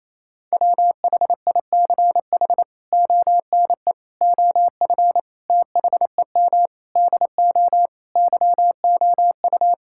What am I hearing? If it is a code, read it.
Morse code, heard as a keyed tone, about 28 wpm, 700 Hz. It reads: WHICH ONE OF THEM DO YOU
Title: Sentences from Top 1000 Words - Review Part 5 - 28wpm